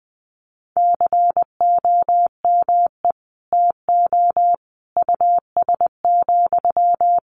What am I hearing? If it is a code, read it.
Morse code, heard as a keyed tone, about 20 wpm, 700 Hz. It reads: COME TO US,
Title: Sentences from Top 1300 Words 20wpm